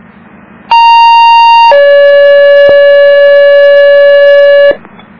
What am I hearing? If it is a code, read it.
Two-tone page, heard as a one-second tone, then a three-second tone, about 930 then 570 Hz, with noise. It is heard from 0.7 to 4.7 s.